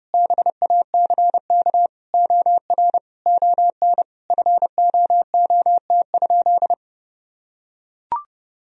Morse code, heard as a keyed tone, about 30 wpm, 700 Hz. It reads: BACK OR ON FOOT?
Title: Sentences from Top 1000 Words - Review Part 3 - 30wpm